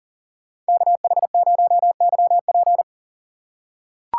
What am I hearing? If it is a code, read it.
Morse code, heard as a keyed tone, about 40 wpm, 700 Hz. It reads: KH0YP
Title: USA Call Signs - 5 Characters - 2x Character Spacing - 40wpm